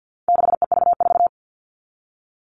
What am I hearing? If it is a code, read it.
Morse code, heard as a keyed tone, about 50 wpm, 700 Hz. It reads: 6EV4
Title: All Cut Numbers - 4-Digit 50wpm